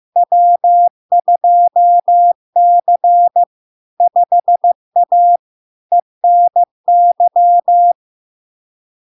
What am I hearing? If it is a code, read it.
Morse code, heard as a keyed tone, about 15 wpm, 700 Hz. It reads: W2C 5A ENY
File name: LARK-FD-2022-Log-in-CW-15wpm-700-Hz